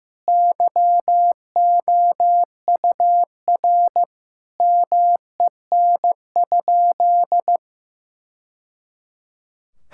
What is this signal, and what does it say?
Morse code, heard as a keyed tone, about 15 wpm, 700 Hz. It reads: YOUR MEN?
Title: Sentences from Top 1000 Words - Review Part 6 - 15wpm